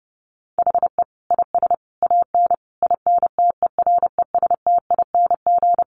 Morse code, heard as a keyed tone, about 28 wpm, 700 Hz. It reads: HE IS AN INTERESTING
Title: Sentences from Top 1000 Words 30wpm